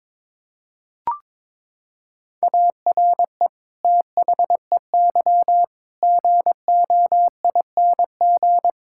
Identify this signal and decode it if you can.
Morse code, heard as a keyed tone, about 22 wpm, 700 Hz. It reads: ARE THEY GOING
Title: Sentences from Top 1600 Words 22wpm